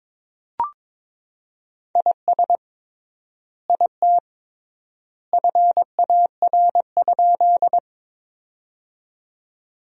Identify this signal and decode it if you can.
Morse code, heard as a keyed tone, about 22 wpm, 700 Hz. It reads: IS IT FAR?
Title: 3x Word-Spacing - Sentences from Top 300 Words - 22wpm